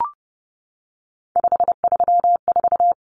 Morse code, heard as a keyed tone, about 30 wpm, 700 Hz. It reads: 534
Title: Numbers - 3-Digit 30wpm